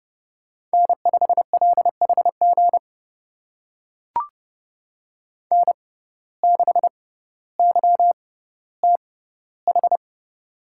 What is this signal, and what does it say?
Morse code, heard as a keyed tone, about 30 wpm, spaced as for 15 wpm, 700 Hz. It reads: N5LHG N6YTH
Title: USA Call Signs - 5 Characters - 6x Character Spacing - 30wpm